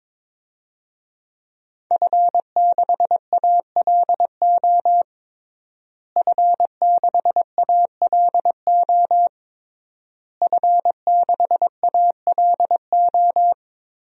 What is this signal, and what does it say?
Morse code, heard as a keyed tone, about 22 wpm, 700 Hz. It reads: F6ALO F6ALO F6ALO
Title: DXCC-100 Call Signs - 5 Characters - Repeated 3x - 22wpm